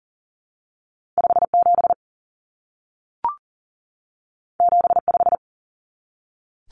Morse code, heard as a keyed tone, about 40 wpm, 700 Hz. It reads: 57 75
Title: Common Cut Numbers - 2-Digit 40wpm